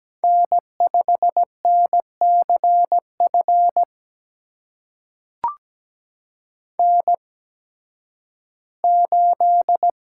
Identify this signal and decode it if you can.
Morse code, heard as a keyed tone, about 17 wpm, spaced as for 11 wpm, 700 Hz. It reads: N5NCF N8
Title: USA Call Signs - 5 Characters - 8x Character Spacing - 17wpm